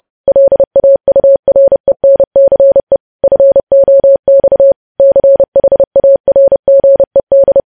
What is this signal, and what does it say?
Morse code, heard as a keyed tone, about 30 wpm, 550 Hz. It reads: LAURENCE FOX CHARGED
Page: news headlines in morse code at 30 wpm